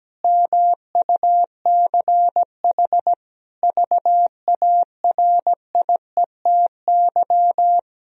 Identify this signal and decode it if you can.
Morse code, heard as a keyed tone, about 17 wpm, 700 Hz. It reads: MUCH VARIETY
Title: Sentences from Top 1000 Words - Review Part 4 - 17wpm